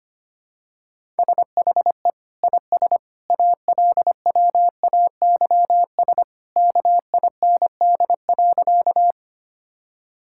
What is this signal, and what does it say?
Morse code, heard as a keyed tone, about 25 wpm, 700 Hz. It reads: SHE IS ALWAYS KIND.
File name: practice / 25wpm / Sentences with 2-Word Phrases - 25wpm